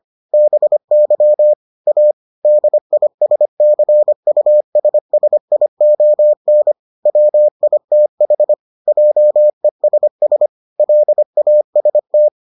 Morse code, heard as a keyed tone, about 25 wpm, 600 Hz. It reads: BY A DISCUSSION WITH JESS LAST